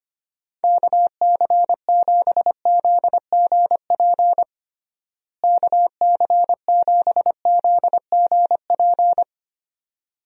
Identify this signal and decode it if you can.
Morse code, heard as a keyed tone, about 25 wpm, 700 Hz. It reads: KC7ZGP KC7ZGP
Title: USA Call Signs - 6 Characters - Repeated 2x - 25wpm